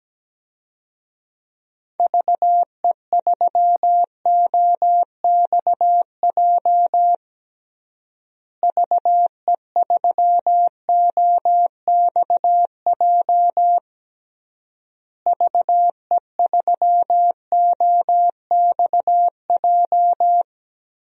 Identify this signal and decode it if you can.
Morse code, heard as a keyed tone, about 17 wpm, 700 Hz. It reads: VE3OXJ VE3OXJ VE3OXJ
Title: DXCC-100 Call Signs - 6 Characters - Repeated 3x - 17wpm